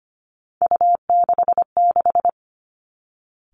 Morse code, heard as a keyed tone, about 25 wpm, 700 Hz. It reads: U66
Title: Three Challenging Letter-Number Combinations 25wpm